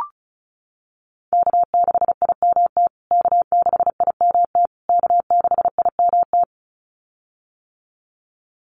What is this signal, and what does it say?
Morse code, heard as a keyed tone, about 34 wpm, 700 Hz. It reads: K6IMT K6IMT K6IMT
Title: USA Call Signs Repeated 3x 35wpm